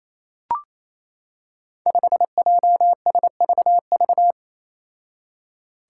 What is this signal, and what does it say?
Morse code, heard as a keyed tone, about 27 wpm, 700 Hz. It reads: 5JSVV
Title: Five Challenging Letter-Number Combinations 28wpm